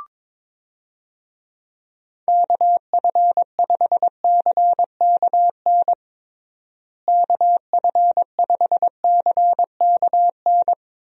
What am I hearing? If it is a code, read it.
Morse code, heard as a keyed tone, about 22 wpm, 700 Hz. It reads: KF5CKN KF5CKN
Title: USA Call Signs - 6 Characters - Repeated 2x - 22wpm